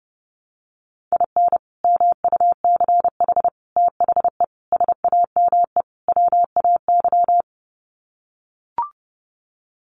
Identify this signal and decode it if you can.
Morse code, heard as a keyed tone, about 30 wpm, 700 Hz. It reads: IN MUCH THE SAME WAY
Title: Sets of 5 Words Encore 30wpm